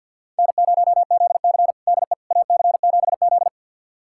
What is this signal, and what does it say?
Morse code, heard as a keyed tone, about 52 wpm, 700 Hz. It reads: N0ZC DE AC7Z
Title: USA Callsign DE Callsign - 4 Characters - 50wpm